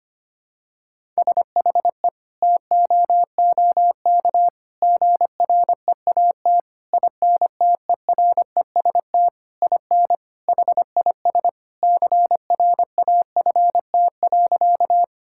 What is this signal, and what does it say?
Morse code, heard as a keyed tone, about 25 wpm, 700 Hz. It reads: SHE TOOK GREAT INTEREST IN HIS CRAFT.
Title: Sentences from Top 1000 Words - Review Part 6 - 25wpm